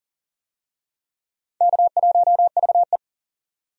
Morse code, heard as a keyed tone, about 40 wpm, 700 Hz. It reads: K1VE
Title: USA Call Signs - 4 Characters - 40wpm